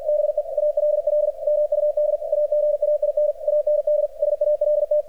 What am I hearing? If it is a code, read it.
Morse code, heard as a keyed tone, about 44 wpm, 600 Hz. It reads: BE ALL MUD AGAIN AND SALT
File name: archived / CRSnet